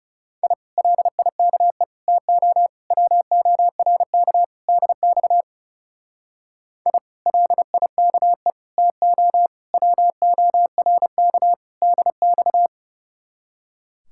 Morse code, heard as a keyed tone, about 37 wpm, 700 Hz. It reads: I LIKE TO WORK DX I LIKE TO WORK DX
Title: Speed-Racing Common QSO Phrases Encore - 30wpm